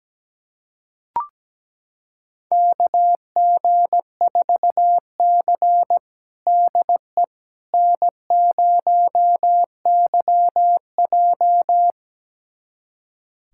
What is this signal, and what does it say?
Morse code, heard as a keyed tone, about 17 wpm, 700 Hz. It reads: KG4C DE N0YJ
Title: USA Callsign DE Callsign - 4 Characters - 17wpm